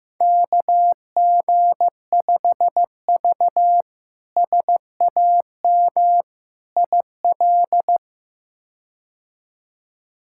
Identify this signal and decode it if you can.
Morse code, heard as a keyed tone, about 15 wpm, 700 Hz. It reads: KG5V SAM IL
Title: K1USN SST QSO Practice - 15wpm